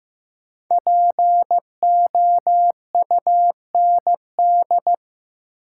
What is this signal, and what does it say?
Morse code, heard as a keyed tone, about 15 wpm, 700 Hz. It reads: POUND
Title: Sentences from Top 1000 Words - Review Part 2 - New Word Drill - 15wpm